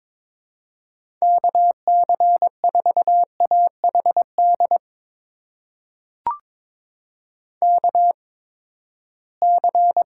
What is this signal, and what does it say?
Morse code, heard as a keyed tone, about 22 wpm, spaced as for 15 wpm, 700 Hz. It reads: KC4AHD KC
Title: USA Call Signs - 6 Characters - 8x Character Spacing - 22wpm